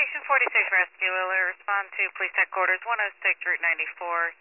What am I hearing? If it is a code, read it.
Emergency Alert System (SAME) tones, heard as harsh buzzing bursts, with speech. It is heard from 0.4 to 3.8 s.